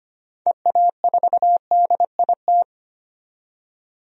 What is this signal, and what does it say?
Morse code, heard as a keyed tone, about 25 wpm, 700 Hz. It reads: EA4DIT
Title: DXCC-100 Call Signs - 6 Characters - 25wpm